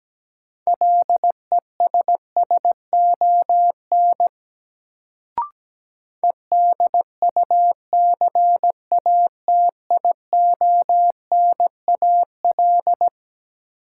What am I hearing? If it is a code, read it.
Morse code, heard as a keyed tone, about 17 wpm, 700 Hz. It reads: LESSON EDUCATIONAL
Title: Sentences from Top 1200 Words - New Word Drill - 17wpm